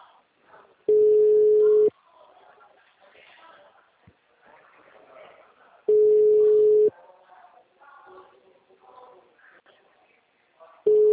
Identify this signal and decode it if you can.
Ringback tone (Europe, 425 Hz), with speech.